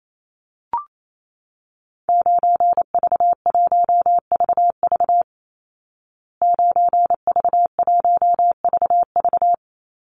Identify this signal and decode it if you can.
Morse code, heard as a keyed tone, about 28 wpm, 700 Hz. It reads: 9V1VV 9V1VV
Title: Call Signs Repeated 2x - 28wpm